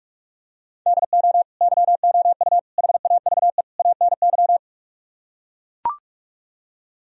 Morse code, heard as a keyed tone, about 45 wpm, 700 Hz. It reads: DO YOU HAVE ANY